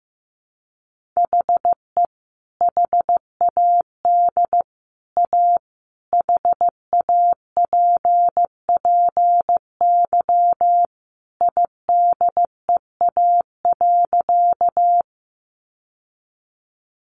Morse code, heard as a keyed tone, about 15 wpm, 700 Hz. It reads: HE HAD A HAPPY IDEA.
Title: Sentences from Top 1000 Words 15wpm